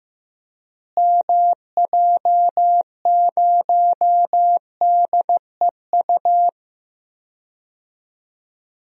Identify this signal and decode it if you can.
Morse code, heard as a keyed tone, about 16 wpm, 700 Hz. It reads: MJ0DEU
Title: DXCC-100 Call Signs - 6 Characters - Repeated 2x - 15wpm